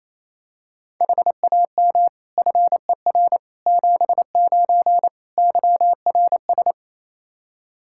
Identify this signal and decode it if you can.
Morse code, heard as a keyed tone, about 27 wpm, 700 Hz. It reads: HAM FER 79 YRS
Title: Practice QSO - Exchange 3 - More Info - 28wpm